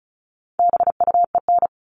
Morse code, heard as a keyed tone, about 34 wpm, 700 Hz. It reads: BUEN
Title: All Cut Numbers - 4-Digit 35wpm